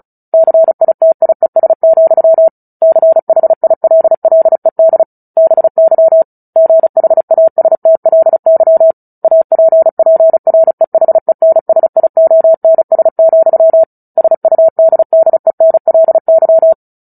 Morse code, heard as a keyed tone, about 34 wpm, 650 Hz. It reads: CITIES, CHILLED BY GHASTLY APPREHENSIONS, SUDDENLY
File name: Book Two / 09-Chapter 9 - 35 WPM